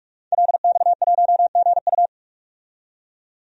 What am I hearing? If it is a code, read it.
Morse code, heard as a keyed tone, about 43 wpm, 700 Hz. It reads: LX1GU
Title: DXCC-100 Call Signs - 5 Characters - 45wpm